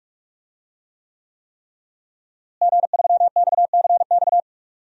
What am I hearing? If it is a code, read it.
Morse code, heard as a keyed tone, about 46 wpm, 700 Hz. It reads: G3XCX
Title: DXCC-100 Call Signs - 5 Characters - Repeated 3x - 45wpm